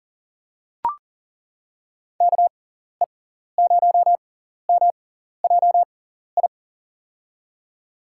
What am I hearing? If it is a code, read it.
Morse code, heard as a keyed tone, about 39 wpm, spaced as for 14 wpm, 700 Hz. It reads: KE0MJI
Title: USA Call Signs - 6 Characters - 6x Character Spacing - 40wpm